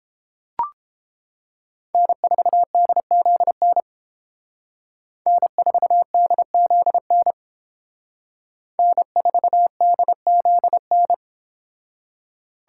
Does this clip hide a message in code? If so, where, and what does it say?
Yes: Morse code from 1.9 to 11.1 s, heard as a keyed tone, about 29 wpm, 700 Hz. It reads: N4DZN N4DZN N4DZN